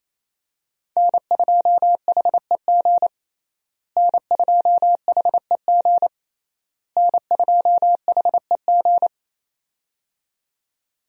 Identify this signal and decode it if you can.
Morse code, heard as a keyed tone, about 28 wpm, 700 Hz. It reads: N2HEG N2HEG N2HEG
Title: USA Call Signs - 5 Characters - Repeated 3x - 28wpm